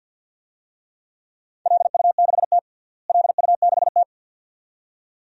Morse code, heard as a keyed tone, about 51 wpm, 700 Hz. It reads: LU6T LU6T
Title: DXCC-100 Call Signs - 4 Characters - Repeated 2x - 50wpm